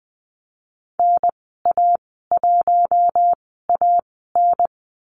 Morse code, heard as a keyed tone, about 20 wpm, spaced as for 15 wpm, 700 Hz. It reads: NA1AN
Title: 042 Lesson 8g - Number 1 - Call Signs 20wpm